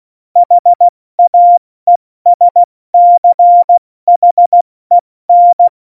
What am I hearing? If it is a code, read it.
Morse code, heard as a keyed tone, about 16 wpm, 700 Hz. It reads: HAESCHEN
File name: Morsekurs / Witze / Witz 13. November 2018, 70 F80